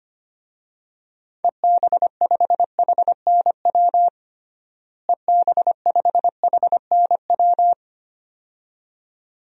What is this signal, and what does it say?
Morse code, heard as a keyed tone, about 25 wpm, 700 Hz. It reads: EB5HNW EB5HNW
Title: DXCC-100 Call Signs - 6 Characters - Repeated 2x - 25wpm